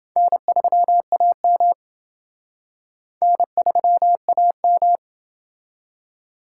Morse code, heard as a keyed tone, about 27 wpm, 700 Hz. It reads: N3AM N3AM